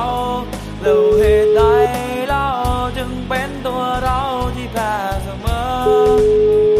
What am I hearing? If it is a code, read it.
Ringback tone (Europe, 425 Hz), with noise.